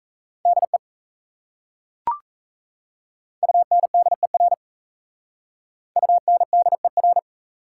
Morse code, heard as a keyed tone, about 35 wpm, 700 Hz. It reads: DE UNDER UNDER
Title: Speed-Racing Prefixes and Suffixes 28wpm